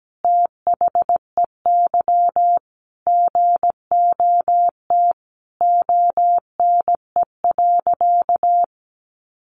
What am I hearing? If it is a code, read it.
Morse code, heard as a keyed tone, about 17 wpm, 700 Hz. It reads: THEY GOT ONE.
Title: Sentences from Top 300 Words 17wpm